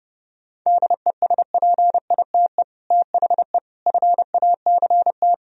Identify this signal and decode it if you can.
Morse code, heard as a keyed tone, about 34 wpm, 700 Hz. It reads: DESPITE THE FACT
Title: Speed-Racing - Sets of 4 Words 30wpm